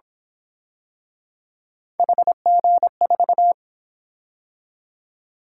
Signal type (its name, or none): Morse code